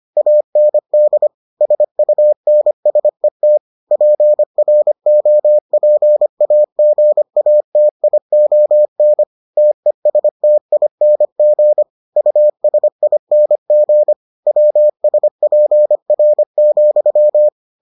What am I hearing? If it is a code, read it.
Morse code, heard as a keyed tone, about 25 wpm, 600 Hz. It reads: AND SUNSET PROPAGATION TESTING USING WSPR,